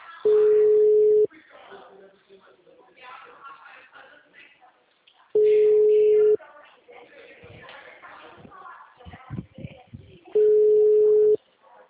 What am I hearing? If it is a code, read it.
Ringback tone (Europe, 425 Hz), with speech.